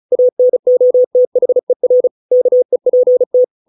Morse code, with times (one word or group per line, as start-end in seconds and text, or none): 0.1-2.1 ANOTHER
2.3-3.4 KEPT